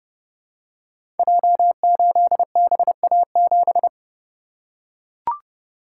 Morse code, heard as a keyed tone, about 28 wpm, 700 Hz. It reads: J8BA7
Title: Code Groups - 5 Letter:Numbers - 30wpm